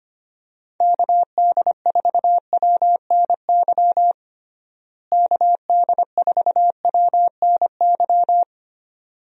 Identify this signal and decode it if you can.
Morse code, heard as a keyed tone, about 25 wpm, 700 Hz. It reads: KD4WNY KD4WNY
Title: USA Call Signs - 6 Characters - Repeated 2x - 25wpm